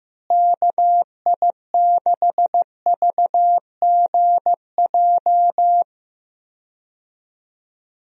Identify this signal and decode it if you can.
Morse code, heard as a keyed tone, about 15 wpm, 700 Hz. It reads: KI6VGJ
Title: USA Call Signs - 6 Characters - Repeated 2x - 15wpm